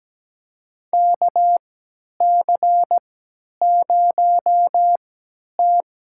Morse code, heard as a keyed tone, about 17 wpm, spaced as for 10 wpm, 700 Hz. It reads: KC0T
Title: USA Call Signs - 3x Character Spacing - 17wpm